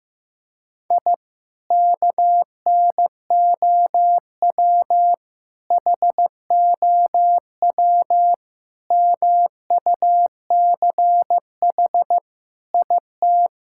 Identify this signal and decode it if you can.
Morse code, heard as a keyed tone, about 15 wpm, 700 Hz. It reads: I KNOW HOW MUCH IT
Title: Sentences from Top 1000 Words - Review Part 5 - 15wpm